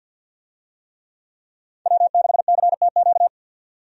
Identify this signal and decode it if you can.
Morse code, heard as a keyed tone, about 50 wpm, 700 Hz. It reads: W6CTQ
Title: USA Call Signs - 5 Characters - 50wpm